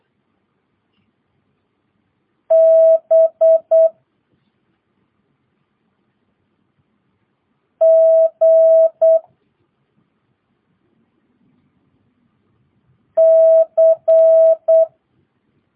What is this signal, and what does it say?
Morse code, heard as a keyed tone, about 8 wpm, 650 Hz. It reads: B G C
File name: jh8cbh / 2022cw